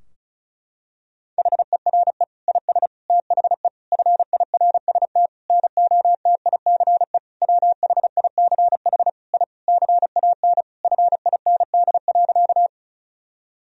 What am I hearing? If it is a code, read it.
Morse code, heard as a keyed tone, about 36 wpm, 700 Hz. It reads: HERE IS THE FIRST NOTICE WHICH I CAN FIND.
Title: Sentences from Top 1000 Words - Review Part 2 - 35wpm